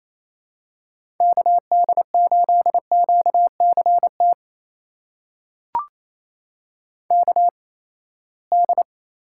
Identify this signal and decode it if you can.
Morse code, heard as a keyed tone, about 28 wpm, spaced as for 19 wpm, 700 Hz. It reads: KD8QCT KD
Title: USA Call Signs - 6 Characters - 8x Character Spacing - 28wpm